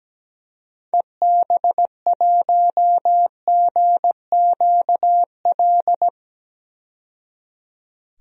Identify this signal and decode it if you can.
Morse code, heard as a keyed tone, about 17 wpm, 700 Hz. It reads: EB1GQL
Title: DXCC-100 Call Signs - 6 Characters - 17wpm